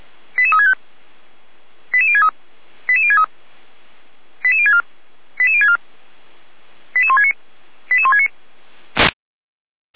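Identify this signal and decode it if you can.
Five-tone selective call (ZVEI), heard as quick runs of five tones, tones of about 70 ms. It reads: 88365 88063 88063 88064 88064 88169 88169
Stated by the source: ZVEI five-tone selective call received by SDR.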